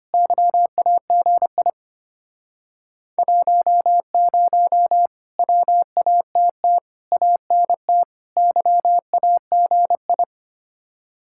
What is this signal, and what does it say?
Morse code, heard as a keyed tone, about 31 wpm, 700 Hz. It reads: YAGI 10 WATT ANT YAGI
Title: Speed-Racing Common QSO Phrases - 25wpm